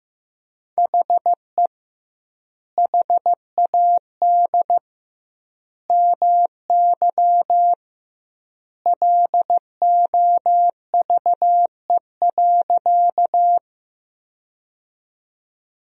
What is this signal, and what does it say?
Morse code, heard as a keyed tone, about 15 wpm, 700 Hz. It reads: HE HAD MY LOVE.
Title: 2x Word-Spacing - Sentences from Top 400 Words - 15wpm